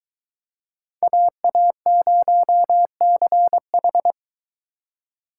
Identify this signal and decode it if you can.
Morse code, heard as a keyed tone, about 22 wpm, 700 Hz. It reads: AA0CH